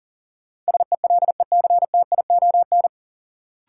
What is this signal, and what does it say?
Morse code, heard as a keyed tone, about 40 wpm, 700 Hz. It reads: SELECTION
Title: Sentences from Top 1400 Words - New Word Drill - 40wpm